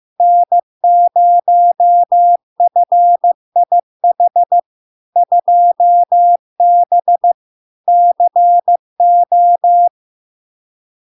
Morse code, heard as a keyed tone, about 15 wpm, 700 Hz. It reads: N0FIH 2B CO
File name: LARK-FD-2022-Log-in-CW-15wpm-700-Hz